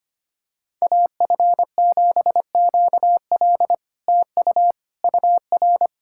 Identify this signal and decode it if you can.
Morse code, heard as a keyed tone, about 26 wpm, 700 Hz. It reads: AF7QL TU UR
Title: SOTA QSO Practice 25wpm